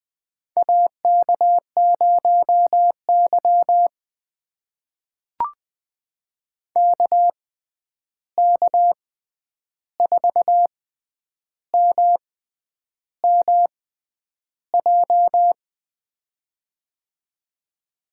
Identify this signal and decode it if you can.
Morse code, heard as a keyed tone, about 20 wpm, spaced as for 9 wpm, 700 Hz. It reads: AK0Y KK4MMJ
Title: USA Call Signs - 6x Character Spacing - 20wpm